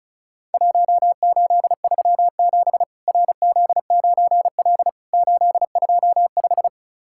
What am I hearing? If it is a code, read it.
Morse code, heard as a keyed tone, about 36 wpm, 700 Hz. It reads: 1837 RZ9L 825
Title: Worked All Europe DX Contest - QTC Exchange - 35wpm